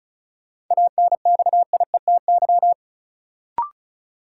Morse code, heard as a keyed tone, about 36 wpm, 700 Hz. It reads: ANXIETY
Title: Sentences from Top 1600 Words - New Word Drill - 35wpm